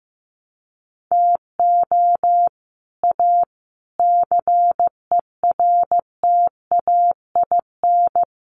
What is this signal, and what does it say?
Morse code, heard as a keyed tone, about 15 wpm, 700 Hz. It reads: TO A CERTAIN
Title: Sentences from Top 1200 Words 15wpm